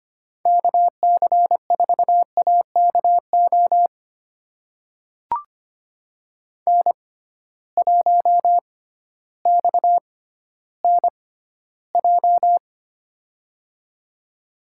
Morse code, heard as a keyed tone, about 25 wpm, spaced as for 14 wpm, 700 Hz. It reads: KC4AKO N1XNJ
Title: USA Call Signs - 6x Character Spacing - 25wpm